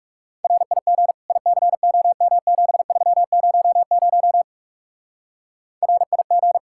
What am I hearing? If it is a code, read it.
Morse code, heard as a keyed tone, about 37 wpm, 700 Hz. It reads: RIG ICOM7300 RIG
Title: Speed-Racing Common QSO Phrases - 30wpm